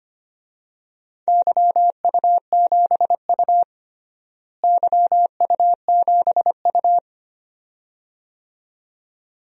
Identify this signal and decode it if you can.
Morse code, heard as a keyed tone, about 25 wpm, 700 Hz. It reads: YU7U YU7U
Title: DXCC-100 Call Signs - 4 Characters - Repeated 2x - 25wpm